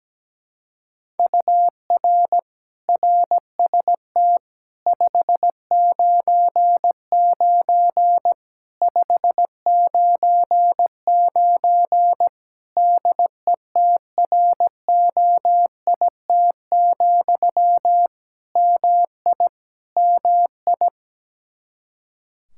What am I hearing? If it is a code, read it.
Morse code, heard as a keyed tone, about 17 wpm, 700 Hz. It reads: UR RST 599 599 DETROIT, MI MI
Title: Practice QSO - Exchange 1 - Beginning - 17wpm